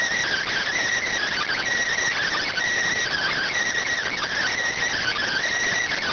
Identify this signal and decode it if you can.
Emergency Alert System (SAME) tones (header bursts), heard as harsh buzzing bursts. It is heard from 1.0 to 5.5 s.